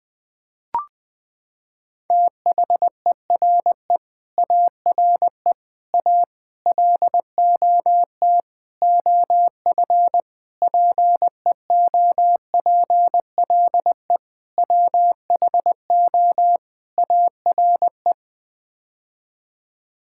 Morse code, heard as a keyed tone, about 21 wpm, 700 Hz. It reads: THERE ARE A LOT OF PEOPLE WHO ARE
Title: Sets of 8 Words - 20wpm